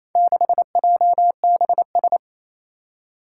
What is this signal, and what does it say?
Morse code, heard as a keyed tone, about 28 wpm, 700 Hz. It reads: 6JBS
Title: Four Challenging Letter-Number Combinations 28wpm